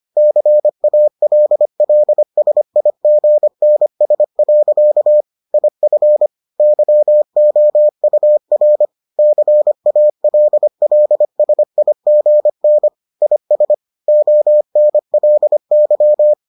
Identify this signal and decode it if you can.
Morse code, heard as a keyed tone, about 25 wpm, 600 Hz. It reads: CALLSIGNS. IF YOUR CALLSIGN IS ONLY